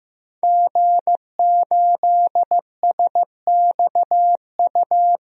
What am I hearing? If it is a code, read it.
Morse code, heard as a keyed tone, about 15 wpm, 700 Hz. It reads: G8SXU
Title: DXCC-100 Call Signs - 5 Characters - Repeated 2x - 15wpm